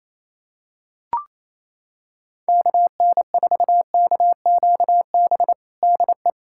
Morse code, heard as a keyed tone, about 27 wpm, 700 Hz. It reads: KN4KQB DE
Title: USA Callsign DE Callsign 28wpm